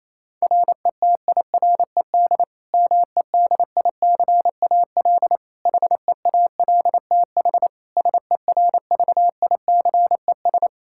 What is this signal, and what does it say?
Morse code, heard as a keyed tone, about 27 wpm, 700 Hz. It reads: RETIRED MEDICAL HEALTH SERVICES
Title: Practice QSO - Exchange 3 - More Info - 28wpm